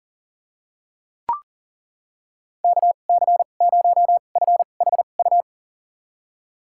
Morse code, heard as a keyed tone, about 40 wpm, spaced as for 29 wpm, 700 Hz. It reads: KC0FHU
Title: USA Call Signs - 6 Characters - 2x Character Spacing - 40wpm